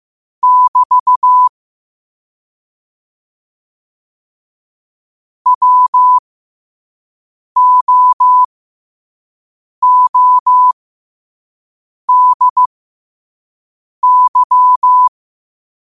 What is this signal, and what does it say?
Morse code, heard as a keyed tone, about 15 wpm, spaced as for 5 wpm, 1000 Hz. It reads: <BT> WOODY
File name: 5wpm